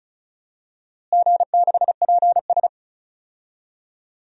Morse code, heard as a keyed tone, about 36 wpm, 700 Hz. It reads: G6PS
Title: DXCC-100 Call Signs - 4 Characters - Repeated 3x - 35wpm